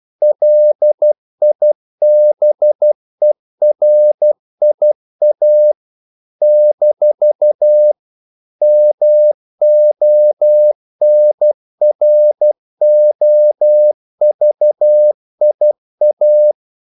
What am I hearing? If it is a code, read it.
Morse code, heard as a keyed tone, about 12 wpm, 600 Hz. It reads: LIBERIA - MONROVIA
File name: CW / Practice / NationalCapitals12